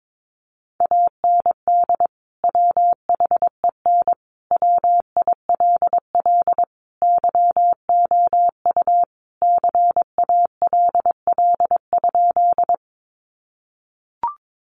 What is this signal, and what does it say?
Morse code, heard as a keyed tone, about 22 wpm, 700 Hz. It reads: AND WHEN WILL YOU CALL?
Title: Sentences from Top 100 Words 22wpm